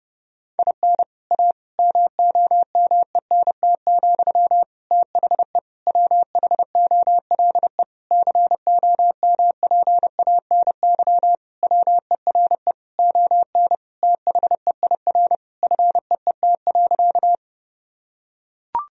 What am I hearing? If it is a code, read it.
Morse code, heard as a keyed tone, about 30 wpm, 700 Hz. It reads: IN A MOMENT, THE WHOLE COMPANY WERE ON THEIR FEET.
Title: Sentences from Top 1000 Words - Review Part 3 - 30wpm